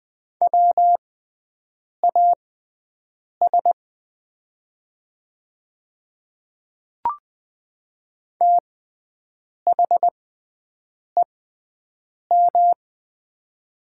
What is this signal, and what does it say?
Morse code, heard as a keyed tone, about 20 wpm, spaced as for 7 wpm, 700 Hz. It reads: WAS THEM
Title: 6x Character Spacing - Top 100 Words - 20wpm